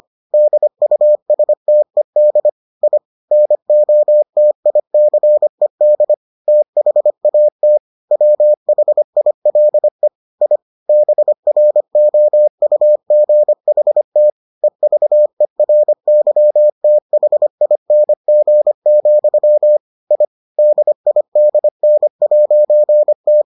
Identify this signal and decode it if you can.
Morse code, heard as a keyed tone, about 25 wpm, 600 Hz. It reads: DUSTED I NOTICED THAT WHILE I BROUGHT EVERYTHING, I DIDN'T